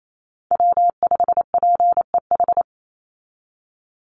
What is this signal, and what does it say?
Morse code, heard as a keyed tone, about 28 wpm, 700 Hz. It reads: W5PEH
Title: Call Signs Repeated 2x - 28wpm